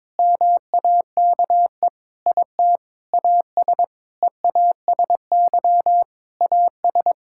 Morse code, heard as a keyed tone, about 22 wpm, 700 Hz. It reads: MAKE IT AS EASY AS
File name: practice / 22wpm / Sentences with 8-Word Phrases - 22wpm